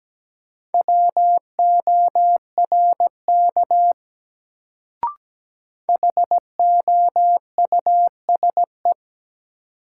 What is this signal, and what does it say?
Morse code, heard as a keyed tone, about 17 wpm, 700 Hz. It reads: WORK HOUSE
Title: Sentences from Top 200 Words - New Word Drill - 17wpm